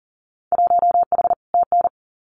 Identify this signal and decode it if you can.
Morse code, heard as a keyed tone, about 42 wpm, 700 Hz. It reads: 1H TN
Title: Winter Field Day Exchanges 40wpm